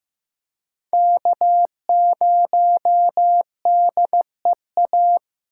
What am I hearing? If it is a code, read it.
Morse code, heard as a keyed tone, about 15 wpm, 700 Hz. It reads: K0DEA
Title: USA Call Signs - 5 Characters - 3x Character Spacing - 15wpm